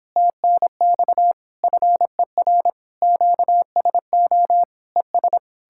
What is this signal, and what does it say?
Morse code, heard as a keyed tone, about 31 wpm, 700 Hz. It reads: TNX FER QSO ES